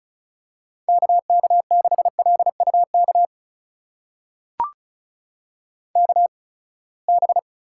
Morse code, heard as a keyed tone, about 36 wpm, spaced as for 24 wpm, 700 Hz. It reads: KK6LUK KB
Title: USA Call Signs - 6 Characters - 8x Character Spacing - 35wpm